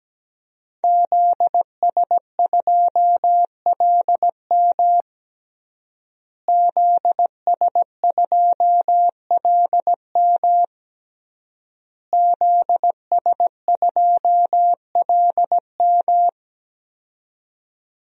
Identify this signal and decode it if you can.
Morse code, heard as a keyed tone, about 17 wpm, 700 Hz. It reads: ZS2LM ZS2LM ZS2LM
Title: DXCC-100 Call Signs - 5 Characters - Repeated 3x - 17wpm